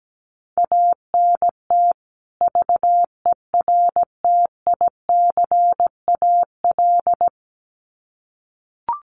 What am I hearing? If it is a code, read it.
Morse code, heard as a keyed tone, about 17 wpm, 700 Hz. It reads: ANT VERTICAL
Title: Common QSO Phrases 17wpm